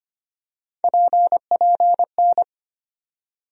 Morse code, heard as a keyed tone, about 25 wpm, 700 Hz. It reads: PPN
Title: USA Call Sign Postfixes - 3 Letters 25wpm